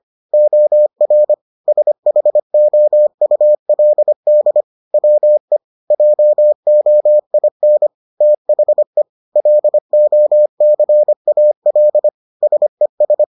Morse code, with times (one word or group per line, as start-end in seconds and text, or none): 0.3-1.3 OR
1.7-4.6 SHOULD
4.9-5.6 WE
5.9-7.9 JOIN
8.2-9.0 THE
9.4-12.1 LOCAL
12.4-13.2 SES